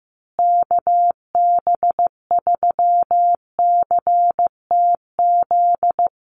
Morse code, heard as a keyed tone, about 15 wpm, 700 Hz. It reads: KB3CTZ
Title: USA Call Signs Repeated 3x 15wpm